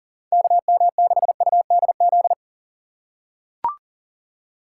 Morse code, heard as a keyed tone, about 40 wpm, 700 Hz. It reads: KM6UDZ
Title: USA Call Signs - 6x Character Spacing - 40wpm